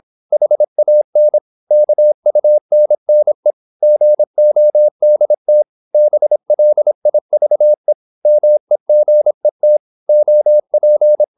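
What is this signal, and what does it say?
Morse code, heard as a keyed tone, about 26 wpm, 600 Hz. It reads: HAN KUNNE GODT BLIVE MEGET OP
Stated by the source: High-speed (QRQ) Morse at 26 wpm.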